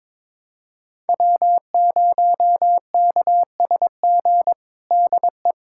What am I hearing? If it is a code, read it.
Morse code, heard as a keyed tone, about 22 wpm, 700 Hz. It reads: W0KSG DE
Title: USA Callsign DE Callsign - 5 Characters - 22wpm